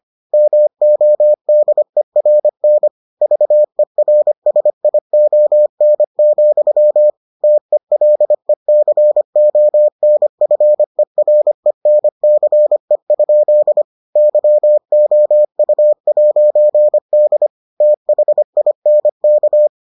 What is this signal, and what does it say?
Morse code, heard as a keyed tone, about 25 wpm, 600 Hz. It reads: MODERN VERSION, TELECONFERENCE? YOU'D THINK